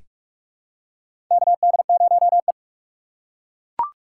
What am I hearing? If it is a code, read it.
Morse code, heard as a keyed tone, about 42 wpm, 700 Hz. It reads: KD0E
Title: USA Call Signs - 4x Character Spacing - 45wpm